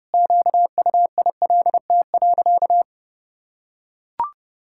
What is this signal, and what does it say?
Morse code, heard as a keyed tone, about 30 wpm, 700 Hz. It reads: QUILT.
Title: Pangram Challenge v2 - 30wpm